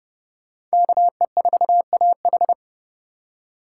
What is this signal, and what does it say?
Morse code, heard as a keyed tone, about 38 wpm, 700 Hz. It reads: KE4AH